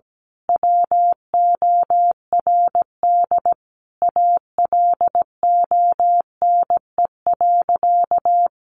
Morse code, heard as a keyed tone, about 17 wpm, 700 Hz. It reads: WORD ALONE.
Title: Sentences from Top 1000 Words 17wpm